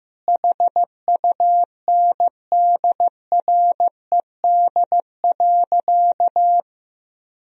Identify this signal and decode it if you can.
Morse code, heard as a keyed tone, about 15 wpm, 700 Hz. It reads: HUNDRED.
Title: Sentences from Top 1000 Words - Review Part 2 - 15wpm